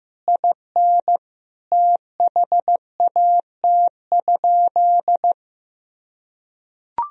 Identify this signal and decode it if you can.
Morse code, heard as a keyed tone, about 15 wpm, 700 Hz. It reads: IN THAT?
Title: Sentences from Top 1000 Words - Review Part 4 - 15wpm